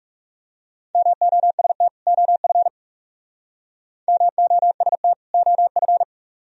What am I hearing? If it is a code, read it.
Morse code, heard as a keyed tone, about 40 wpm, 700 Hz. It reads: MOST OF MOST OF